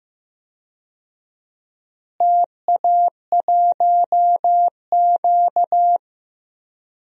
Morse code, heard as a keyed tone, about 15 wpm, 700 Hz. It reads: TA1Q
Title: DXCC-100 Call Signs - 4 Characters - Repeated 2x - 15wpm